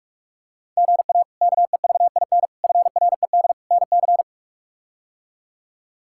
Morse code, heard as a keyed tone, about 41 wpm, 700 Hz. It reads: GA KEVIN FRED NC
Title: K1USN SST QSO Practice - 45wpm